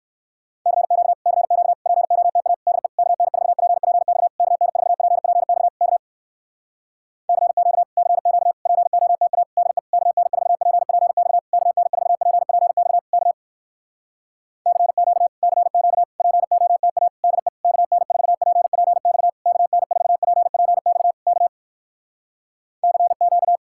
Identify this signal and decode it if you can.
Morse code, heard as a keyed tone, about 59 wpm, 700 Hz. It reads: CQ CQ POTA DE KN4PPX KN4PPX K CQ CQ POTA DE KN4PPX KN4PPX K CQ CQ POTA DE KN4PPX KN4PPX K CQ